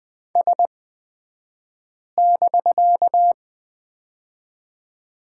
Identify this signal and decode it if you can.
Morse code, heard as a keyed tone, about 20 wpm, spaced as for 5 wpm, 700 Hz. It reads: S<BK>
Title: 249 Lesson 40d - BK - Two Characters 20wpm